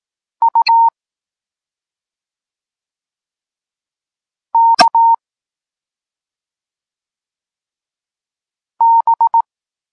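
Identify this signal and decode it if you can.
Morse code, heard as a keyed tone, about 18 wpm, 920 Hz. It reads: U K B
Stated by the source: Polish Morse lesson teaching the letter C, from SP9MOA Niepołomicki Klub Krótkofalowców.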